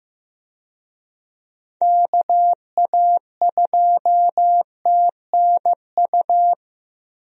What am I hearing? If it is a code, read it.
Morse code, heard as a keyed tone, about 15 wpm, 700 Hz. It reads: KA2TNU
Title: USA Call Signs - 6 Characters - Repeated 2x - 15wpm